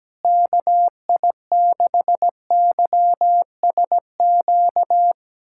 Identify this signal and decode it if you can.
Morse code, heard as a keyed tone, about 17 wpm, 700 Hz. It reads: KI6YSQ
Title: USA Callsign DE Callsign - 6 Characters - 17wpm